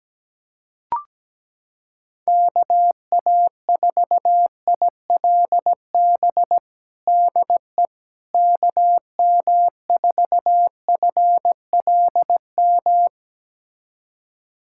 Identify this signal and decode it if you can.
Morse code, heard as a keyed tone, about 17 wpm, 700 Hz. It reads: KA4ILB DE KM4FLM
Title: USA Callsign DE Callsign 17wpm